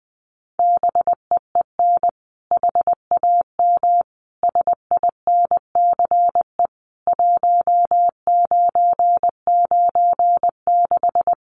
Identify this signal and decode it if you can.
Morse code, heard as a keyed tone, about 20 wpm, 700 Hz. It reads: BEEN HAM SINCE 1996
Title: Common QSO Phrases 20wpm